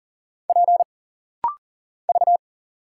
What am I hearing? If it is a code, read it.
Morse code, heard as a keyed tone, about 40 wpm, 700 Hz. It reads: P V
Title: Single Letter-Number Alt - Rapid-Fire 40wpm